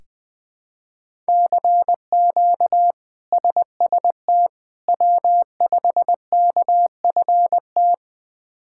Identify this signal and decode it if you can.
Morse code, heard as a keyed tone, about 20 wpm, 700 Hz. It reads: CQ SST W5KFT
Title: K1USN SST QSO Practice - 20wpm